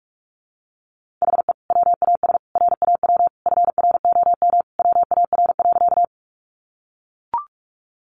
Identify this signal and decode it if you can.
Morse code, heard as a keyed tone, about 44 wpm, 700 Hz. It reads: HE WAS RAW FROM WAR.
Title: Sentences from Top 1500 Words 45wpm